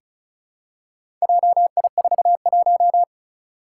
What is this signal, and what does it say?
Morse code, heard as a keyed tone, about 36 wpm, 700 Hz. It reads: JI41
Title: Four Challenging Letter-Number Combinations 35wpm